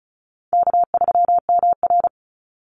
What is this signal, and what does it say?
Morse code, heard as a keyed tone, about 36 wpm, 700 Hz. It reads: K3MR
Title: Call Signs - 35wpm